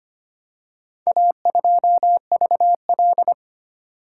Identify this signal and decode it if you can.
Morse code, heard as a keyed tone, about 25 wpm, 690 Hz. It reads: A2VL
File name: practice / 25wpm / Four Letter-Number Combinations - 25wpm